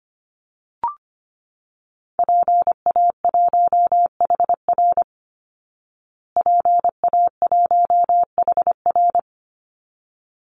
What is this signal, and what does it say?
Morse code, heard as a keyed tone, about 25 wpm, 700 Hz. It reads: PA1HR PA1HR
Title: Call Signs Repeated 2x - 25wpm